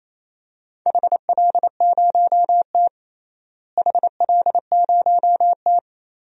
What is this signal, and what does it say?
Morse code, heard as a keyed tone, about 28 wpm, 700 Hz. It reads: HL0T HL0T
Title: DXCC-100 Call Signs - 4 Characters - Repeated 3x - 28wpm